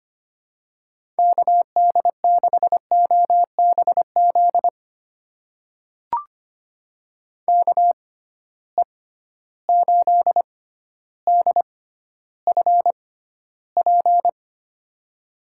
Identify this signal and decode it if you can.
Morse code, heard as a keyed tone, about 25 wpm, spaced as for 13 wpm, 700 Hz. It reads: KD6OBZ KE8DFP
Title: USA Call Signs - 6 Characters - 6x Character Spacing - 25wpm